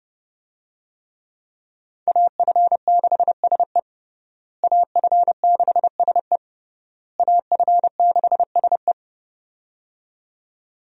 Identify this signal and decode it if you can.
Morse code, heard as a keyed tone, about 32 wpm, 700 Hz. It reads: AF6SE AF6SE AF6SE
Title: USA Call Signs - 5 Characters - Repeated 3x - 30wpm